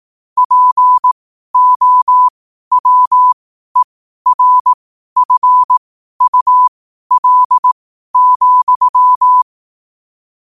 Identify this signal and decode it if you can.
Morse code, heard as a keyed tone, about 18 wpm, spaced as for 12 wpm, 1000 Hz. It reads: POWERFUL,
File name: qotdfiles / 12wpm